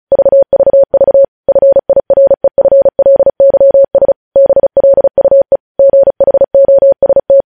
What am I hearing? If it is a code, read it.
Morse code, heard as a keyed tone, about 36 wpm, 550 Hz. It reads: VVV FIREFLYS BLUE GHOST
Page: News Headlines in Morse Code at 35 WPM